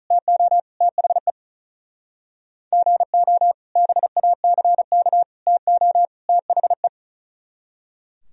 Morse code, heard as a keyed tone, about 47 wpm, 700 Hz. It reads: TO THE GO BACK TO THE